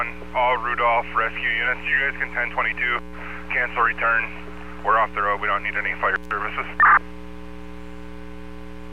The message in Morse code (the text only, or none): none